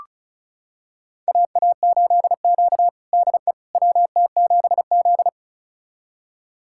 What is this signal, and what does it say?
Morse code, heard as a keyed tone, about 36 wpm, 700 Hz. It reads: AA8Q DE WT7Z
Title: USA Callsign DE Callsign - 4 Characters - 35wpm